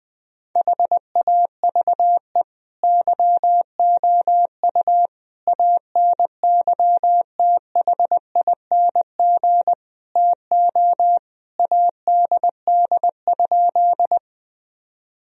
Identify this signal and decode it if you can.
Morse code, heard as a keyed tone, about 21 wpm, 700 Hz. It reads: HAVE YOU ANYTHING TO ADD?
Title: Sentences from Top 1000 Words - Review Part 1 - 20wpm